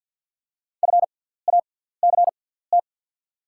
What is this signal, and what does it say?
Morse code, heard as a keyed tone, about 53 wpm, spaced as for 17 wpm, 700 Hz. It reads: FACT
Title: 6x Character Spacing - Top 500 Words - 50wpm